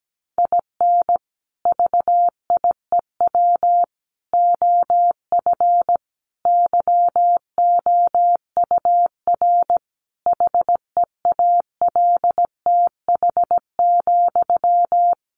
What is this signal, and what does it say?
Morse code, heard as a keyed tone, about 17 wpm, 700 Hz. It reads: IN VIEW OF YOUR HEALTH,
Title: Sentences from Top 900 Words 17wpm